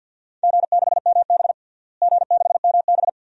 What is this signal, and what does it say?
Morse code, heard as a keyed tone, about 47 wpm, 700 Hz. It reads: G6MB G6MB
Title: DXCC-100 Call Signs - 4 Characters - Repeated 2x - 50wpm